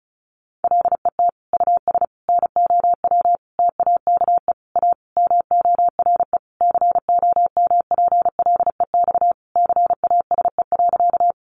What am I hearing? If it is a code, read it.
Morse code, heard as a keyed tone, about 34 wpm, 700 Hz. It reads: LET US NOW TAKE A MORE COMPLEX CASE.
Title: Sentences from Top 900 Words 35wpm